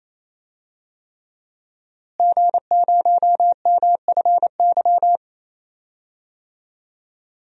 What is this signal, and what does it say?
Morse code, heard as a keyed tone, about 28 wpm, 700 Hz. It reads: G0MFY
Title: DXCC-100 Call Signs - 5 Characters - Repeated 3x - 28wpm